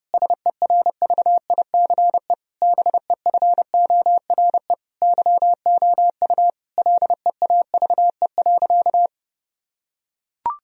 Morse code, heard as a keyed tone, about 30 wpm, 700 Hz. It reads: SERVICE BEFORE YOU LEAVE.
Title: Sentences from Top 1000 Words - Review Part 4 - 30wpm